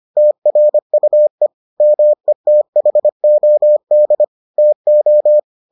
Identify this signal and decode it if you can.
Morse code, heard as a keyed tone, about 25 wpm, 600 Hz. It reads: TRUE METHOD TO